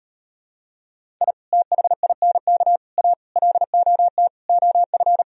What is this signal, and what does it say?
Morse code, heard as a keyed tone, about 38 wpm, 700 Hz. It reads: I THINK A LOT OF